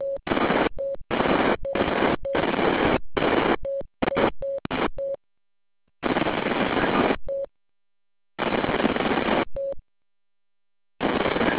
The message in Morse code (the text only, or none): none